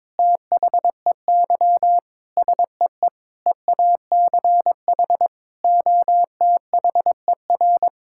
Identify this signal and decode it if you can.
Morse code, heard as a keyed tone, about 22 wpm, 700 Hz. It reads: THEY SEE EACH OTHER
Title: Sentences from Top 1000 Words - Review Part 5 - 22wpm